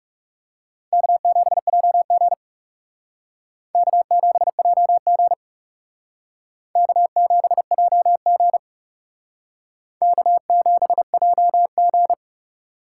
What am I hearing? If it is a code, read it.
Morse code, heard as a keyed tone, about 38 wpm, 700 Hz. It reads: K7JG K7JG K7JG K7JG